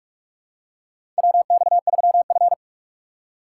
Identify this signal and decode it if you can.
Morse code, heard as a keyed tone, about 47 wpm, 700 Hz. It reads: WX3F
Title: USA Call Signs - 4 Characters - 3x Character Spacing - 45wpm